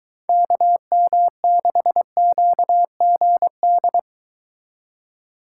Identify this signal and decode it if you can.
Morse code, heard as a keyed tone, about 23 wpm, 700 Hz. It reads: KM6QGD